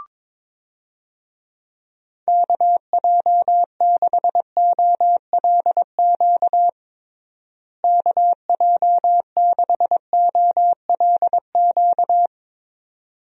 Morse code, heard as a keyed tone, about 22 wpm, 700 Hz. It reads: KJ6OLQ KJ6OLQ
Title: USA Call Signs - 6 Characters - Repeated 2x - 22wpm